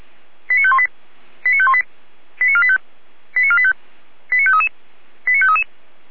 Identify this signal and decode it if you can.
Five-tone selective call (ZVEI), heard as quick runs of five tones, tones of about 70 ms. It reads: 89518 89518 89485 89485 89533 89533
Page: SDR/ZVEI